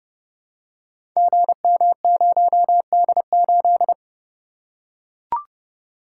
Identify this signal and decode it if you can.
Morse code, heard as a keyed tone, about 29 wpm, 700 Hz. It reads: GM0D8
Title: Code Groups - 5 Letter:Numbers - 30wpm